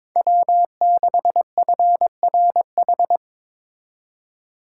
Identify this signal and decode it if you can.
Morse code, heard as a keyed tone, about 26 wpm, 700 Hz. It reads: W6FRH
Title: Speed Racing - USA 1x3 Call Signs 22wpm